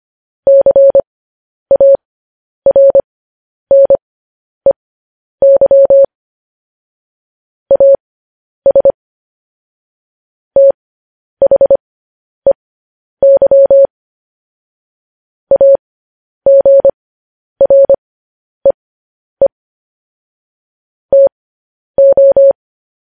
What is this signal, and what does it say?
Morse code, heard as a keyed tone, about 25 wpm, spaced as for 10 wpm, 550 Hz. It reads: CARNEY AS THEY AGREE TO
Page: news headlines in morse code at 10 wpm